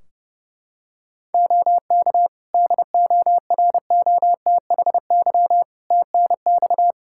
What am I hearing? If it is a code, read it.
Morse code, heard as a keyed tone, about 30 wpm, 700 Hz. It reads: OK DOROTHY TNX
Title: Practice QSO - Exchange 4 - Ending - 30wpm